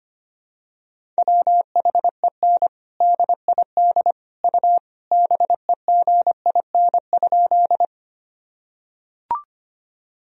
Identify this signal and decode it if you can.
Morse code, heard as a keyed tone, about 25 wpm, 700 Hz. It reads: WHEN DID U BEGIN?
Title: Sentences from Top 300 Words and Abbreviations 25wpm